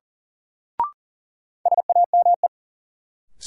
Morse code, heard as a keyed tone, about 40 wpm, 700 Hz. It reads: SAME
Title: Sets of 2 Words - New Word Drill - 40wpm